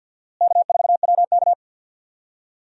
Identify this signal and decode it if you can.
Morse code, heard as a keyed tone, about 53 wpm, 700 Hz. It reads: K4WK
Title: USA Call Signs - 4 Characters - Repeated 2x - 50wpm